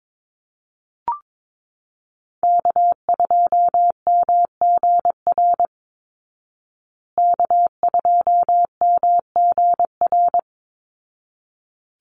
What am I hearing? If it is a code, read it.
Morse code, heard as a keyed tone, about 22 wpm, 700 Hz. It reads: K2MGR K2MGR
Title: Call Signs Repeated 3x - 22wpm